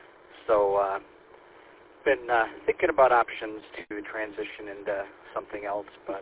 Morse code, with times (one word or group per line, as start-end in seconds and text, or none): none